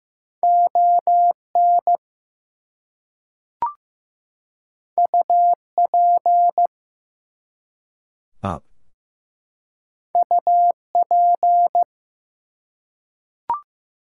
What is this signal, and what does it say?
Morse code, heard as a keyed tone, about 15 wpm, 700 Hz. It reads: ON UP UP
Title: Two Letter Words - 15wpm